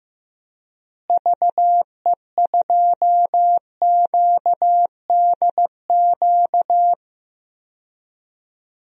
Morse code, heard as a keyed tone, about 15 wpm, 700 Hz. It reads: VE2QDQ
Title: DXCC-100 Call Signs - 6 Characters - Repeated 2x - 15wpm